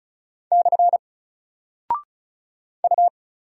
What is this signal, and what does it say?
Morse code, heard as a keyed tone, about 36 wpm, 700 Hz. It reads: / U
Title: Single Character (Letter:Number:Punctuation:Prosign) Alt 35wpm